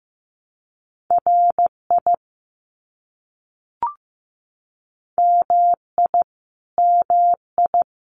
Morse code, heard as a keyed tone, about 15 wpm, 700 Hz. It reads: RI MI MI
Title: US States Repeated 3x 15wpm